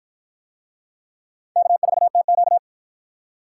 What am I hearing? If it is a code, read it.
Morse code, heard as a keyed tone, about 53 wpm, 700 Hz. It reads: K4TQ